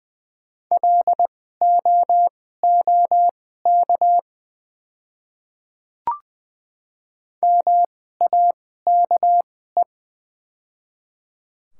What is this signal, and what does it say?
Morse code, heard as a keyed tone, about 20 wpm, spaced as for 14 wpm, 700 Hz. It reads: LOOK MAKE
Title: 2x Character Spacing - Top 100 Words - 20wpm